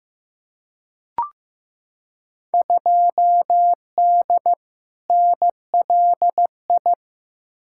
Morse code, heard as a keyed tone, about 15 wpm, 700 Hz. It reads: 2D NLI
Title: Field Day Exchanges - 15wpm